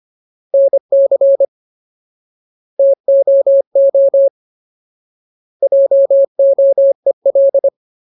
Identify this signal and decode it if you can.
Morse code, heard as a keyed tone, about 25 wpm, 550 Hz. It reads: NC TOO JOEL
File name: Ultimate Word List / 25WPM Ultimate